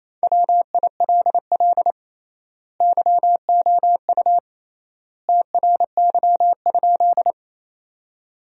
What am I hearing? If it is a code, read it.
Morse code, heard as a keyed tone, about 28 wpm, 700 Hz. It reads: WILL YOU TRY?
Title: 3x Word-Spacing - Sentences from Top 200 Words - 28wpm